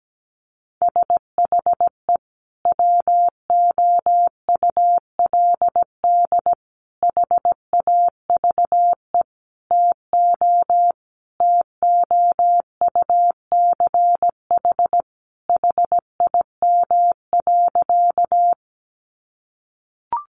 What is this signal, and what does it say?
Morse code, heard as a keyed tone, about 17 wpm, 700 Hz. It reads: SHE WOULD HAVE TO TOUCH HIM.
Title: Sentences from Top 1200 Words 17wpm